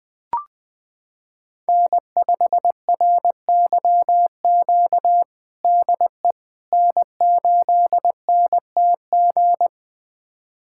Morse code, heard as a keyed tone, about 21 wpm, 700 Hz. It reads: N5RYQ DE N8NTG
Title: USA Callsign DE Callsign - 5 Characters - 20wpm